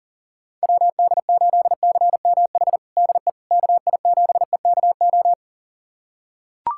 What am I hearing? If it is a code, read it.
Morse code, heard as a keyed tone, about 37 wpm, 700 Hz. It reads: WD8CMH DE KI7EKO
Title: USA Callsign DE Callsign 40wpm